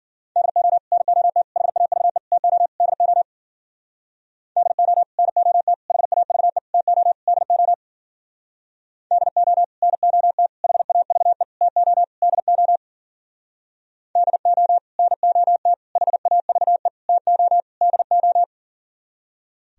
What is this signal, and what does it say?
Morse code, heard as a keyed tone, about 52 wpm, 700 Hz. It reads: DO NOT HAVE TO DO DO NOT HAVE TO DO DO NOT HAVE TO DO DO NOT HAVE TO DO